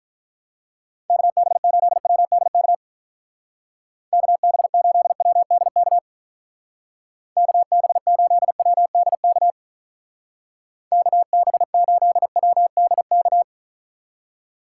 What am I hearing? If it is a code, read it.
Morse code, heard as a keyed tone, about 44 wpm, 700 Hz. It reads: KB8WDK KB8WDK KB8WDK KB8WDK